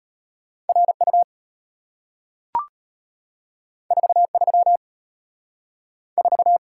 Morse code, heard as a keyed tone, about 34 wpm, 700 Hz. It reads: RU 43 4